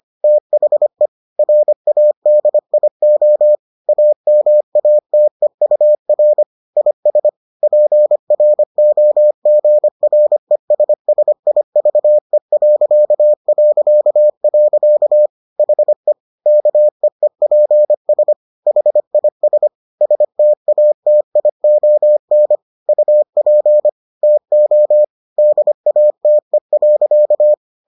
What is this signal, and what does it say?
Morse code, heard as a keyed tone, about 25 wpm, 600 Hz. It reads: THE RADIO AMATEUR IS PROGRESSIVE... HE KEEPS HIS STATION UP TO DATE.